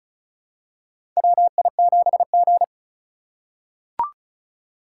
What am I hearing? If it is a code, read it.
Morse code, heard as a keyed tone, about 36 wpm, 700 Hz. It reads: WI7G
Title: USA Call Signs - 4 Characters - 3x Character Spacing - 35wpm